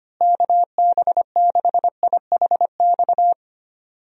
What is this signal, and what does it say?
Morse code, heard as a keyed tone, about 25 wpm, 700 Hz. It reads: KB6IHX
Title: USA Call Signs - 6 Characters - Repeated 3x - 25wpm